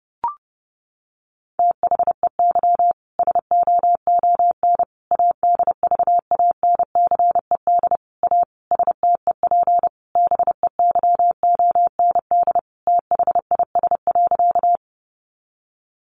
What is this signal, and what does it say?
Morse code, heard as a keyed tone, about 30 wpm, 700 Hz. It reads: THEY SOON ADVANCED A STEP BEYOND THIS.
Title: Sentences from Top 1200 Words 30wpm